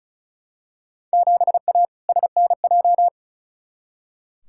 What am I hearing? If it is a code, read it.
Morse code, heard as a keyed tone, about 36 wpm, 700 Hz. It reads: 7A SNJ
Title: Field Day Exchanges Encore - 35wpm